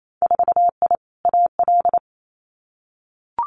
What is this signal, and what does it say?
Morse code, heard as a keyed tone, about 28 wpm, 700 Hz. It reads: 4I AL
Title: Winter Field Day Exchanges Encore 28wpm